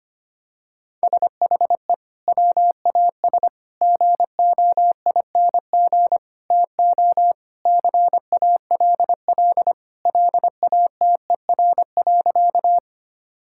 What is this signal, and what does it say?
Morse code, heard as a keyed tone, about 25 wpm, 700 Hz. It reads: SHE WAS GOING TO CALL LATER.
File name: practice / 25wpm / Sentences with 4-Word Phrases - 25wpm